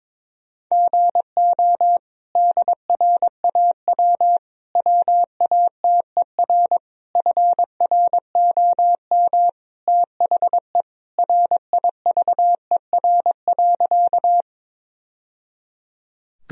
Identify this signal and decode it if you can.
Morse code, heard as a keyed tone, about 22 wpm, 700 Hz. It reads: GO DRAW WATER FROM THE RIVER.
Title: Sentences from Top 1000 Words - Review Part 1 - 22wpm